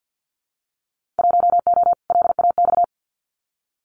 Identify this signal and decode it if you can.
Morse code, heard as a keyed tone, about 50 wpm, 700 Hz. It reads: 1O LAX
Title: Winter Field Day Exchanges 50wpm